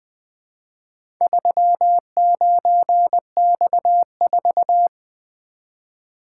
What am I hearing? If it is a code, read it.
Morse code, heard as a keyed tone, about 20 wpm, 700 Hz. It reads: 39X4
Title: Four Letter-Number Combinations - 20wpm